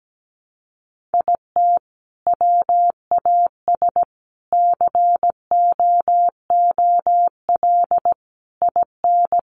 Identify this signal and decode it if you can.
Morse code, heard as a keyed tone, about 17 wpm, 700 Hz. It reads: IT WAS COOL IN
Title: Sentences from Top 1200 Words 17wpm